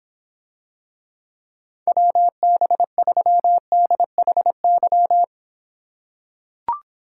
Morse code, heard as a keyed tone, about 26 wpm, 700 Hz. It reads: WB3DHY